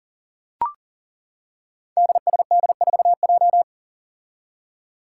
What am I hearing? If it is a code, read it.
Morse code, heard as a keyed tone, about 40 wpm, 700 Hz. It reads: DSD4J
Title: Five Challenging Letter-Number Combinations 40wpm